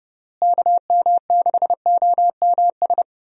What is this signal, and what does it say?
Morse code, heard as a keyed tone, about 30 wpm, 700 Hz. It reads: KM6OMS
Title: USA Call Signs - 6 Characters - Repeated 2x - 30wpm